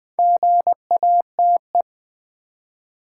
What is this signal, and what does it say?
Morse code, heard as a keyed tone, about 20 wpm, 690 Hz. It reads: GATE
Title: Build A Word - 2 Letters At A Time - 4-Letter Words - 20wpm